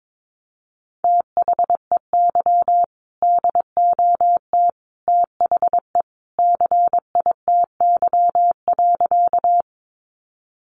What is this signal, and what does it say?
Morse code, heard as a keyed tone, about 22 wpm, 700 Hz. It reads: THEY DOT THE CITY.
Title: Sentences from Top 1500 Words 22wpm